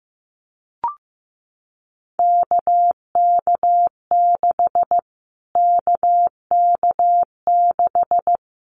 Morse code, heard as a keyed tone, about 15 wpm, 700 Hz. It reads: KK6 KK6
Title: Callsign Prefixes for Top 25 DXCC - Repeated 3x - 15wpm